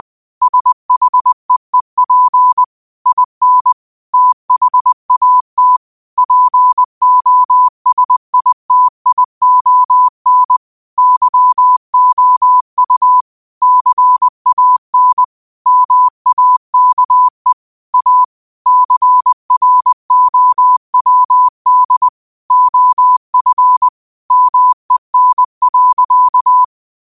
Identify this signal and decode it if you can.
Morse code, heard as a keyed tone, about 20 wpm, 1000 Hz. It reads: SHEEP IN THAT POSITION YOU CAN MAKE A CROWD OF MEN.
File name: qotdfiles / 20wpm